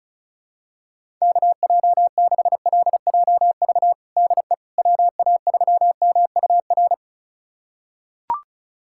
Morse code, heard as a keyed tone, about 36 wpm, 700 Hz. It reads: KJ6LJV DE WA3MUR
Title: USA Callsign DE Callsign - 6 Characters - 35wpm